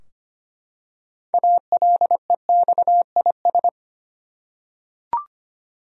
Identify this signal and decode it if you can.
Morse code, heard as a keyed tone, about 25 wpm, 700 Hz. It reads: ALEXIS
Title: USA Common Names - 25wpm